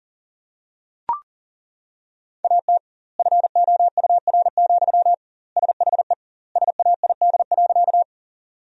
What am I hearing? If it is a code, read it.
Morse code, heard as a keyed tone, about 40 wpm, 700 Hz. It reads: AT FOUR, SHE SAID.
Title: 2x Word-Spacing - Sentences from Top 300 Words - 40wpm